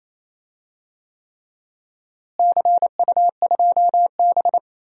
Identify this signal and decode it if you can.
Morse code, heard as a keyed tone, about 28 wpm, 700 Hz. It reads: CU2B
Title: DXCC-100 Call Signs - 4 Characters - Repeated 3x - 28wpm